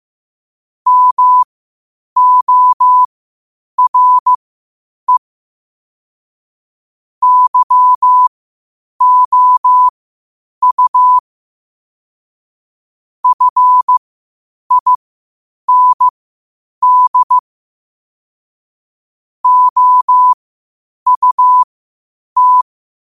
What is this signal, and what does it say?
Morse code, heard as a keyed tone, about 15 wpm, spaced as for 8 wpm, 1000 Hz. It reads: MORE YOU FIND OUT
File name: qotdfiles / 8wpm